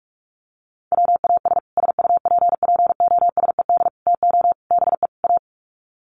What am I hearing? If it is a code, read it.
Morse code, heard as a keyed tone, about 44 wpm, 700 Hz. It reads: WAS SUPPOSED TO BE A
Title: Sets of 5 Words Encore 45wpm